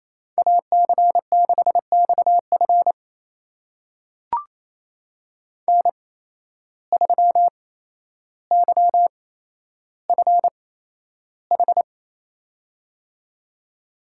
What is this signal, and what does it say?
Morse code, heard as a keyed tone, about 28 wpm, spaced as for 12 wpm, 700 Hz. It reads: AC6XF N3YFH
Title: USA Call Signs - 5 Characters - 8x Character Spacing - 28wpm